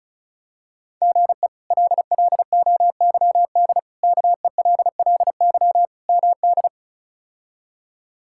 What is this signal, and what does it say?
Morse code, heard as a keyed tone, about 36 wpm, 700 Hz. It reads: GE LLOYD KELLY MD
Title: K1USN SST QSO Practice - 35wpm